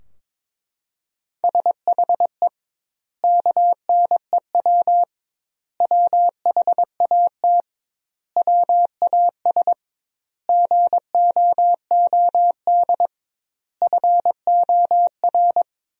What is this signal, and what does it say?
Morse code, heard as a keyed tone, about 22 wpm, 700 Hz. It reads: SHE KNEW WHAT WAS GOOD FOR
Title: 2x Word-Spacing - Sentences from Top 400 Words - 22wpm